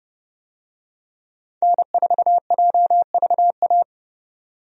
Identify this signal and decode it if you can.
Morse code, heard as a keyed tone, about 30 wpm, 700 Hz. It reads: N4JVA
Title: USA Call Signs - 5 Characters - 30wpm